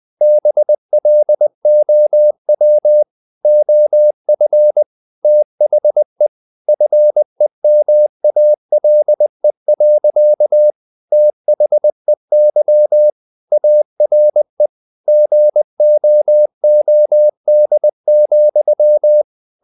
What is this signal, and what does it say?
Morse code, heard as a keyed tone, about 20 wpm, 600 Hz. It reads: BLOW OF THE FEMALE. THEY ARE GOOD,
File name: the old man and the sea / the old man and the sea part 1 20wpm